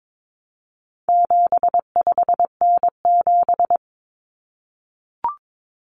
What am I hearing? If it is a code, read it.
Morse code, heard as a keyed tone, about 22 wpm, 700 Hz. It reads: 75N7
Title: Common Cut Numbers - 4-Digit 22wpm